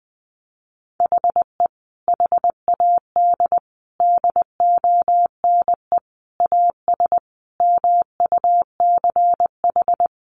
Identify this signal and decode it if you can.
Morse code, heard as a keyed tone, about 20 wpm, 700 Hz. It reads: HE HAD DONE AS MUCH
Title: Sentences from Top 1200 Words 20wpm